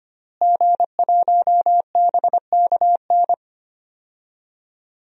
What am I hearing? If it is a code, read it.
Morse code, heard as a keyed tone, about 25 wpm, 700 Hz. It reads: G1BKN
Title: DXCC-100 Call Signs - 5 Characters - Repeated 3x - 25wpm